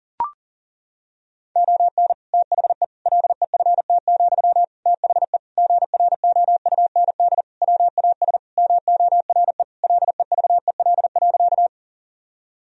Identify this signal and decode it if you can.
Morse code, heard as a keyed tone, about 40 wpm, 700 Hz. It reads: ON THE LEFT, THE GROUND WAS MORE LEVEL.
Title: Sentences from Top 1000 Words - Review Part 4 - 40wpm